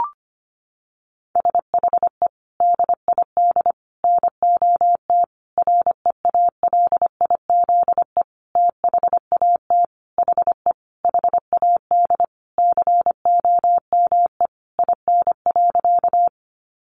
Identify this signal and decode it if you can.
Morse code, heard as a keyed tone, about 25 wpm, 700 Hz. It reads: SHE DID NOT REALIZE THAT HE HAD COME IN.
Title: Sentences from Top 1000 Words 25wpm